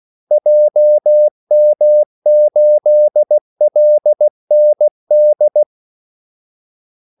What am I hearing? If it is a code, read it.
Morse code, heard as a keyed tone, about 16 wpm, 600 Hz. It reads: JM8LND
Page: Kaldesignaler 16 wpm